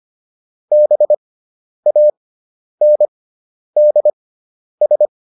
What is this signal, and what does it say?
Morse code, heard as a keyed tone, about 25 wpm, spaced as for 10 wpm, 600 Hz. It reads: BANDS